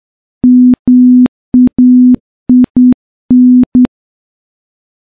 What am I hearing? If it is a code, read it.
Morse code, heard as a keyed tone, about 10 wpm, 250 Hz. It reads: MAIN
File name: morse / morse-training-10WPM